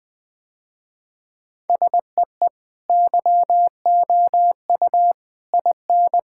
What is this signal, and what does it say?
Morse code, heard as a keyed tone, about 20 wpm, 700 Hz. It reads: SEE YOU IN
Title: Sentences with 3-Word Phrases - Spoken Beginning - 20wpm